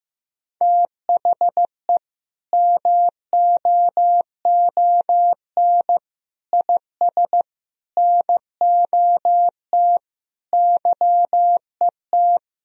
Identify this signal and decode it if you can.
Morse code, heard as a keyed tone, about 15 wpm, 700 Hz. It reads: THE MOON IS NOT YET
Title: Sentences from Top 1000 Words - Review Part 3 - 15wpm